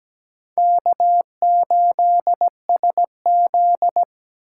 Morse code, heard as a keyed tone, about 20 wpm, 700 Hz. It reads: K8SZ
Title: Speed Racing - USA Call Signs 17wpm